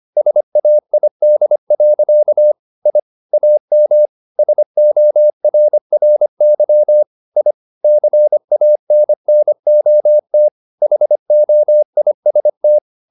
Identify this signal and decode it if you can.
Morse code, heard as a keyed tone, about 25 wpm, 600 Hz. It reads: SAID. I AM SORRY I CANNOT HOIST
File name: the old man and the sea part 1 25wpm